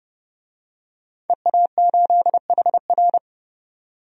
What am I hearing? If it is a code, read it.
Morse code, heard as a keyed tone, about 30 wpm, 700 Hz. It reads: EA8HR
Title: DXCC-100 Call Signs - 5 Characters - 30wpm